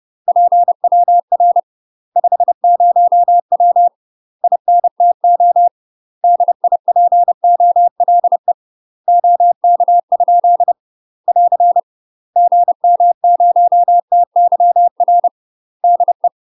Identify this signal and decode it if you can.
Morse code, heard as a keyed tone, about 30 wpm, 700 Hz. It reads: PWR 50W INTO DIPOLE OK? <AR> GM0TYR DE
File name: UK MASTER QSOs TRAINING / 30WPM / UK MASTER QSOs 0016